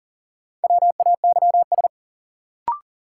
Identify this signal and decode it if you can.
Morse code, heard as a keyed tone, about 40 wpm, 700 Hz. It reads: WAYS
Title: Sets of 5 Words Encore - New Word Drill - 40wpm